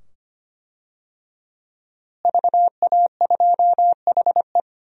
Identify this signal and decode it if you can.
Morse code, heard as a keyed tone, about 25 wpm, 700 Hz. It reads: VA2HE
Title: DXCC-100 Call Signs - 5 Characters - Repeated 2x - 25wpm